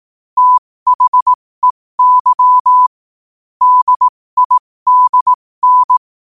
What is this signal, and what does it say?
Morse code, heard as a keyed tone, about 18 wpm, spaced as for 15 wpm, 1000 Hz. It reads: THEY DIDN
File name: qotdfiles / 15wpm